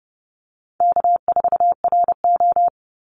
Morse code, heard as a keyed tone, about 30 wpm, 700 Hz. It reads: K4RO
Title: Easy Call Signs - 30wpm